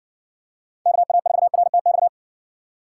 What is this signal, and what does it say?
Morse code, heard as a keyed tone, about 60 wpm, 700 Hz. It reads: KA4RTX